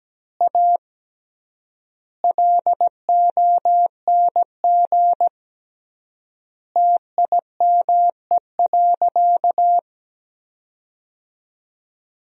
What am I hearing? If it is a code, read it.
Morse code, heard as a keyed tone, about 17 wpm, 700 Hz. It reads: A LONG TIME.
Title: 3x Word-Spacing - Sentences from Top 400 Words - 17wpm